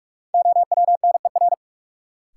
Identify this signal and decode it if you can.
Morse code, heard as a keyed tone, about 46 wpm, 700 Hz. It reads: OWNER
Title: Sentences from Top 1600 Words - New Word Drill Alt - 45wpm